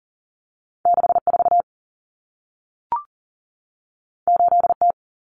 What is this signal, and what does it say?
Morse code, heard as a keyed tone, about 40 wpm, 700 Hz. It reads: 64 8T
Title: Common Cut Numbers - 2-Digit 40wpm